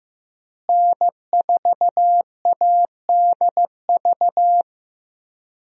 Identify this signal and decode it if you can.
Morse code, heard as a keyed tone, about 15 wpm, 700 Hz. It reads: N4ADV
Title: USA Call Signs - 5 Characters - Repeated 3x - 15wpm